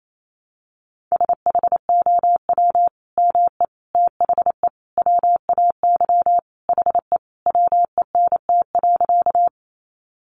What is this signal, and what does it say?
Morse code, heard as a keyed tone, about 28 wpm, 700 Hz. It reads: SHOW ME THE WAY HE WENT.
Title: Sentences from Top 200 Words 28wpm